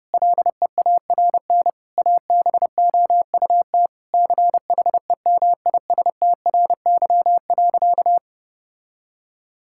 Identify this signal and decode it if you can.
Morse code, heard as a keyed tone, about 30 wpm, 700 Hz. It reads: LEARN ABOUT CHEMISTRY.
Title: Sentences from Top 1000 Words - Review Part 6 - 30wpm